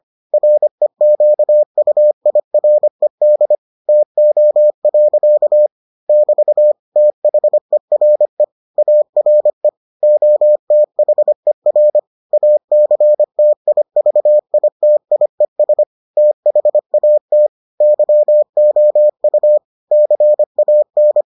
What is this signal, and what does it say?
Morse code, heard as a keyed tone, about 25 wpm, 600 Hz. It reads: REQUIRED TO. <BT> THERE ARE OTHER ACTIVITIES THAT YOU CAN